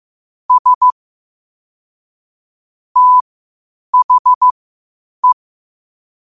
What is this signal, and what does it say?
Morse code, heard as a keyed tone, about 15 wpm, spaced as for 8 wpm, 1000 Hz. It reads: S THE
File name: 8wpm